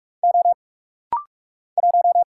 Morse code, heard as a keyed tone, about 46 wpm, 700 Hz. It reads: O 1
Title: Single Letter-Number Alt - Rapid-Fire 45wpm